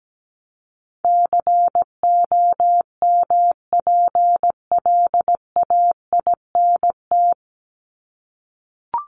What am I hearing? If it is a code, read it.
Morse code, heard as a keyed tone, about 17 wpm, 700 Hz. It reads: COMPLAINT
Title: Top 1251-1500 Words 17wpm